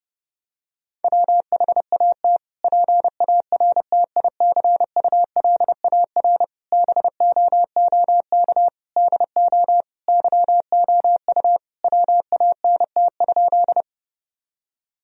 Morse code, heard as a keyed tone, about 32 wpm, 700 Hz. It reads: WHAT PARTICULAR BOOK DO YOU WANT?
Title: Sentences from Top 1000 Words - Review Part 4 - 30wpm